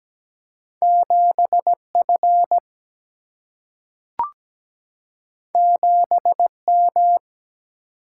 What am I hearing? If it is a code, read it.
Morse code, heard as a keyed tone, about 17 wpm, 700 Hz. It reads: 7 F 7 M
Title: Single-Letter Single-Number 17wpm